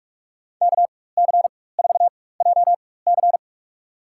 Morse code, heard as a keyed tone, about 46 wpm, spaced as for 21 wpm, 700 Hz. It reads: KC4JC
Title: USA Call Signs - 4x Character Spacing - 45wpm